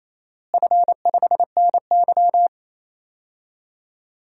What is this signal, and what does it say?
Morse code, heard as a keyed tone, about 27 wpm, 700 Hz. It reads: F5NY
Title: DXCC-100 Call Signs - 4 Characters - Repeated 2x - 28wpm